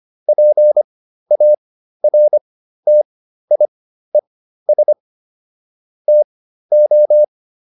Morse code, heard as a keyed tone, about 25 wpm, spaced as for 13 wpm, 600 Hz. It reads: PARTIES TO